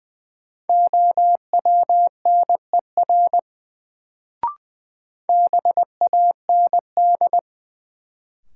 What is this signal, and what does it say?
Morse code, heard as a keyed tone, about 20 wpm, 700 Hz. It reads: OWNER BAND
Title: Sentences from Top 1600 Words - New Word Drill - 20wpm